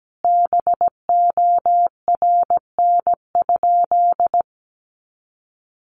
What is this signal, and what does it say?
Morse code, heard as a keyed tone, about 17 wpm, 700 Hz. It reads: BORN?
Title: Sentences from Top 1000 Words 17wpm